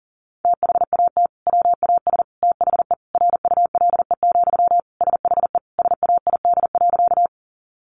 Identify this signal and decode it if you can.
Morse code, heard as a keyed tone, about 40 wpm, 700 Hz. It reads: THAT WAS THE RULE, SHE SAID.
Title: Sentences from Top 400 Words 40wpm